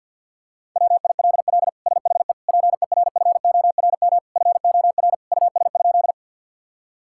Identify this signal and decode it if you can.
Morse code, heard as a keyed tone, about 50 wpm, 700 Hz. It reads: WILL SHE PERFORM FOR US?
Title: Sentences from Top 1000 Words - Review Part 5 - 50wpm